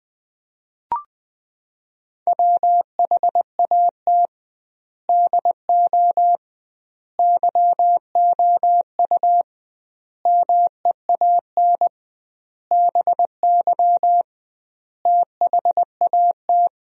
Morse code, heard as a keyed tone, about 20 wpm, 700 Hz. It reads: WHAT DO YOU MEAN BY THAT
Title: 2x Word-Spacing - Sentences from Top 400 Words - 20wpm